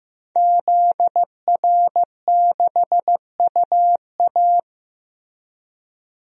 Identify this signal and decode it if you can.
Morse code, heard as a keyed tone, about 15 wpm, 700 Hz. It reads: ZR6UA
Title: DXCC-100 Call Signs - 5 Characters - 15wpm